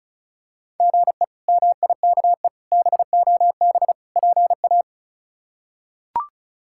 Morse code, heard as a keyed tone, about 34 wpm, 700 Hz. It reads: GE MIKE BOB PA
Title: K1USN SST QSO Practice - 35wpm